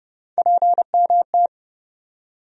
Morse code, heard as a keyed tone, about 30 wpm, 700 Hz. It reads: PMT
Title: USA Call Sign Postfixes - 3 Letters 30wpm